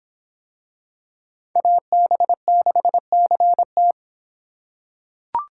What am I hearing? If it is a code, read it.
Morse code, heard as a keyed tone, about 24 wpm, 700 Hz. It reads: AB6CT